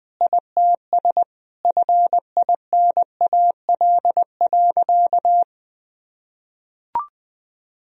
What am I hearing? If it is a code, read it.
Morse code, heard as a keyed tone, about 20 wpm, 700 Hz. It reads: ITS FINAL.
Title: Sentences from Top 1000 Words - Review Part 2 - 20wpm